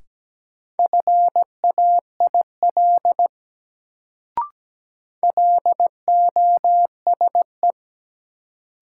Morse code, heard as a keyed tone, about 17 wpm, 700 Hz. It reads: FAIL LOSE
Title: Sentences from Top 1100 Words - New Word Drill - 17wpm